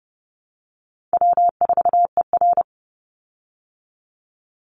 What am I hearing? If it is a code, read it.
Morse code, heard as a keyed tone, about 30 wpm, 700 Hz. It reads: W4ER
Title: Call Signs Repeated 3x - 30wpm